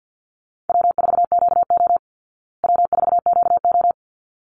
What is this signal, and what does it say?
Morse code, heard as a keyed tone, about 50 wpm, 700 Hz. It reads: W4QO W4QO
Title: Easy Call Signs Repeated 2x - 50wpm